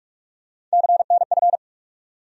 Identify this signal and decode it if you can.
Morse code, heard as a keyed tone, about 42 wpm, 700 Hz. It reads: CNF
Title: USA Call Sign Postfixes - 3 Letters 45wpm